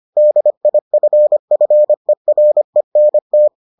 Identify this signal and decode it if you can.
Morse code, heard as a keyed tone, about 25 wpm, 600 Hz. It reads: DIFFERENT